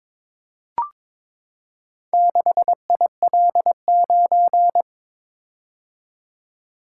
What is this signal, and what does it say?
Morse code, heard as a keyed tone, about 22 wpm, 690 Hz. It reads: 6IL9
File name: practice / 22wpm / Four Letter-Number Combinations - 22wpm